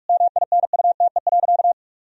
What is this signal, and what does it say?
Morse code, heard as a keyed tone, about 41 wpm, 700 Hz. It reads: MINUTE.
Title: Sentences from Top 1000 Words - Review Part 2 - 45wpm